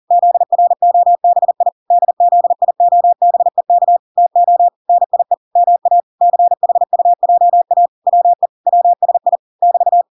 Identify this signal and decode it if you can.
Morse code, heard as a keyed tone, about 40 wpm, 700 Hz. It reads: ZROBI DZIOBEK TO NIE MA CHUJA WE WSI <BT>